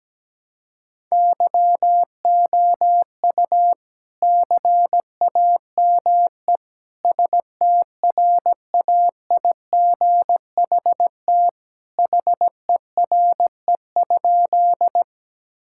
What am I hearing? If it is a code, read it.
Morse code, heard as a keyed tone, about 17 wpm, 700 Hz. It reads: YOU CAME STRAIGHT HERE?
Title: Sentences from Top 1000 Words - Review Part 6 - 17wpm